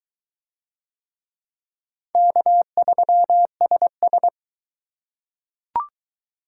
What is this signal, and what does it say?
Morse code, heard as a keyed tone, about 22 wpm, 700 Hz. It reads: K3SS